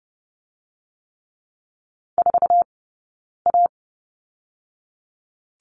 Morse code, heard as a keyed tone, about 30 wpm, 700 Hz. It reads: 4 A
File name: course / 30:10wpm / 099 Lesson 17d - Letter u - Two Characters 30wpm